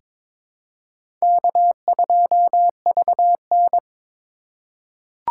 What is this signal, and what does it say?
Morse code, heard as a keyed tone, about 22 wpm, 700 Hz. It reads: K2VN
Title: USA Call Signs - 4 Characters - 4x Character Spacing - 22wpm